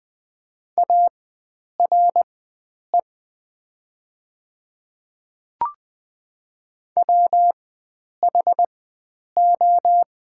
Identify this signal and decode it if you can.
Morse code, heard as a keyed tone, about 20 wpm, spaced as for 9 wpm, 700 Hz. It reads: ARE WHO
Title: 4x Character Spacing - Top 100 Words - 20wpm